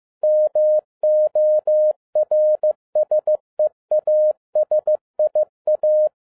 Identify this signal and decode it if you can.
Morse code, heard as a keyed tone, about 15 wpm, 610 Hz. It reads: MORSE ASIA
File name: 2012-01-mp3 / n13-A NEWLY DESIGNED QSL CARD-15wpm